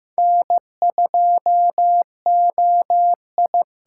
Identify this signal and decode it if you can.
Morse code, heard as a keyed tone, about 15 wpm, 700 Hz. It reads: N2OI
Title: USA Call Signs - 5 Characters - 6x Character Spacing - 15wpm